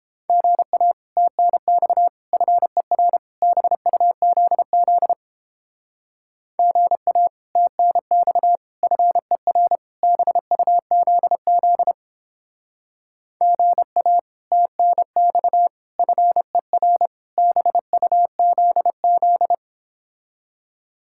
Morse code, heard as a keyed tone, about 31 wpm, 700 Hz. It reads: GA TNX FER BUZZ GA TNX FER BUZZ GA TNX FER BUZZ